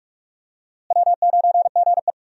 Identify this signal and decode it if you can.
Morse code, heard as a keyed tone, about 47 wpm, 700 Hz. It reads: W9GE
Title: USA Call Signs - 4 Characters - Repeated 2x - 45wpm